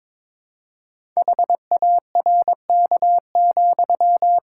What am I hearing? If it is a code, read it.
Morse code, heard as a keyed tone, about 22 wpm, 700 Hz. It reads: HARK,
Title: Pangram Challenge v2 - 22wpm